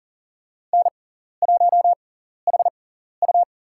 Morse code, heard as a keyed tone, about 40 wpm, spaced as for 14 wpm, 700 Hz. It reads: N1HU
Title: USA Call Signs - 6x Character Spacing - 40wpm